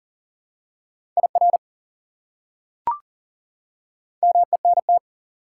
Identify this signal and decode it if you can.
Morse code, heard as a keyed tone, about 40 wpm, 700 Hz. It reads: IR MENT
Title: Prefixes and Suffixes 40wpm